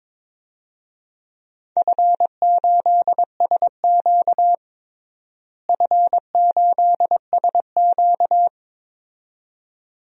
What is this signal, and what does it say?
Morse code, heard as a keyed tone, about 22 wpm, 700 Hz. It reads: F8SQ F8SQ
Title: DXCC-100 Call Signs - 4 Characters - Repeated 2x - 22wpm